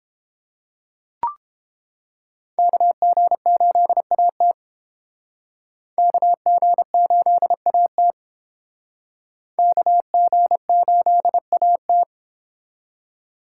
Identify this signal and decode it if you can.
Morse code, heard as a keyed tone, about 30 wpm, 700 Hz. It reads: KG8AT KG8AT KG8AT